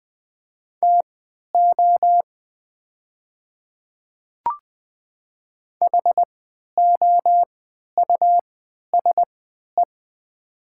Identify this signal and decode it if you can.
Morse code, heard as a keyed tone, about 20 wpm, spaced as for 11 wpm, 700 Hz. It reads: TO HOUSE
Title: 3x Character Spacing - Top 300 Words - 20wpm